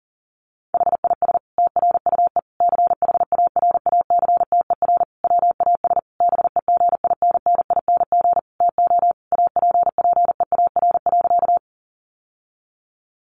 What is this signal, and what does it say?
Morse code, heard as a keyed tone, about 40 wpm, 700 Hz. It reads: HIS TRUE CHARACTER WAS BEGINNING TO APPEAR.
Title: Sentences from Top 1200 Words 40wpm